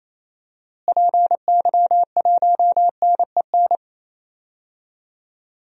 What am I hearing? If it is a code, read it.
Morse code, heard as a keyed tone, about 27 wpm, 700 Hz. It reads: PY1NEN
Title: DXCC-100 Call Signs - 6 Characters - Repeated 3x - 28wpm